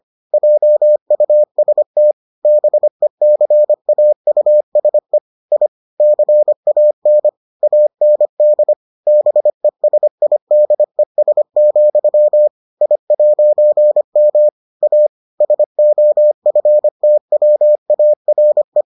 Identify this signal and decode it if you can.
Morse code, heard as a keyed tone, about 25 wpm, 600 Hz. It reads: JUST BECAUSE I CAN AND BESIDES, I'M A SOFTWARE